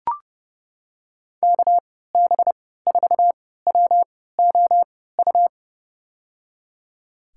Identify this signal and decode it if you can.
Morse code, heard as a keyed tone, about 32 wpm, spaced as for 17 wpm, 700 Hz. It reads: KB4WOU
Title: USA Call Signs - 6 Characters - 3x Character Spacing - 30wpm